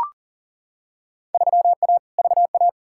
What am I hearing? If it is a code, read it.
Morse code, heard as a keyed tone, about 40 wpm, 700 Hz. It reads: 3A VA
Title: Field Day Exchanges Encore - 40wpm